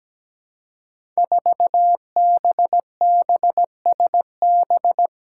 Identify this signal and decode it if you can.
Morse code, heard as a keyed tone, about 17 wpm, 700 Hz. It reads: 4BBSB
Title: Five Challenging Letter-Number Combinations 17wpm